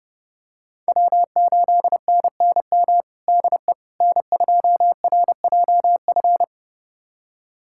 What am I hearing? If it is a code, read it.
Morse code, heard as a keyed tone, about 30 wpm, 700 Hz. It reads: W8NNM DE N2RJF
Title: USA Callsign DE Callsign - 5 Characters - 30wpm